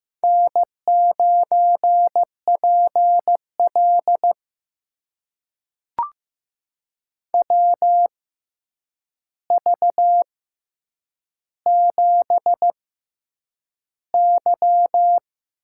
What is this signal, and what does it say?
Morse code, heard as a keyed tone, about 15 wpm, spaced as for 8 wpm, 700 Hz. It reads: N9PL WV7Y
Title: USA Call Signs - 4 Characters - 6x Character Spacing - 15wpm